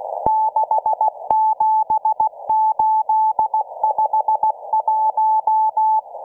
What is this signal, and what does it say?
Morse code, heard as a keyed tone, about 16 wpm, 870 Hz. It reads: 67851